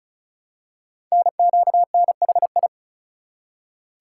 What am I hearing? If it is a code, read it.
Morse code, heard as a keyed tone, about 36 wpm, 700 Hz. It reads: NQNHI
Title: Code Groups - 5 Letters - 35wpm